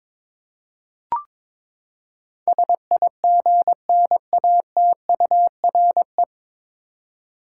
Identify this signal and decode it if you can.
Morse code, heard as a keyed tone, about 22 wpm, 700 Hz. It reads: SIGNATURE
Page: Morse Code Ninja practice 22wpm